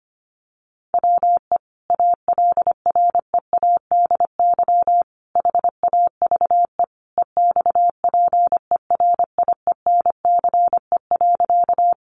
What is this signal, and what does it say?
Morse code, heard as a keyed tone, about 25 wpm, 700 Hz. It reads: WE ALREADY HAVE EXPERIENCE.
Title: Sentences from Top 700 Words 25wpm